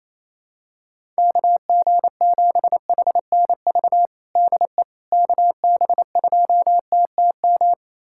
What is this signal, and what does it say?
Morse code, heard as a keyed tone, about 28 wpm, 700 Hz. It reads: KG7HNV DE KB2TTM
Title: USA Callsign DE Callsign - 6 Characters - 28wpm